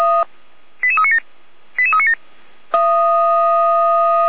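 Five-tone selective call (ZVEI), heard as quick runs of five tones, tones of about 70 ms. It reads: 88397 88397
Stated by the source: Five-tone ZVEI selective call received by SDR, from audio.f85360.feuerwehr-orlamuende.org, a fire brigade (Feuerwehr Orlamünde) server.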